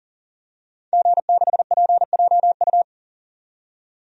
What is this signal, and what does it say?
Morse code, heard as a keyed tone, about 40 wpm, 700 Hz. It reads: G6PJU
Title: DXCC-100 Call Signs - 5 Characters - 40wpm